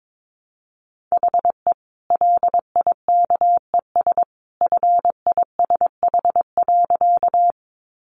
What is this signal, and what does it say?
Morse code, heard as a keyed tone, about 22 wpm, 700 Hz. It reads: HE LIKES FISH.
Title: Easy Phrases 22wpm